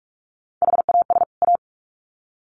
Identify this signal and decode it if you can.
Morse code, heard as a keyed tone, about 46 wpm, 700 Hz. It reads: HAS A
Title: Sets of 2 Words 45wpm